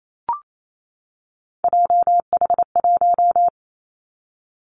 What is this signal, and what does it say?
Morse code, heard as a keyed tone, about 28 wpm, 700 Hz. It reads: JH1
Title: Callsign Prefixes for Top 25 DXCC - 28wpm